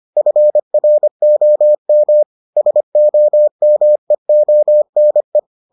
Morse code, heard as a keyed tone, about 25 wpm, 600 Hz. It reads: FROM SOMEONE